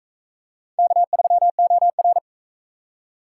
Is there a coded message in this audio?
Yes: Morse code from 0.8 to 2.2 s, heard as a keyed tone, about 41 wpm, 700 Hz. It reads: K3OR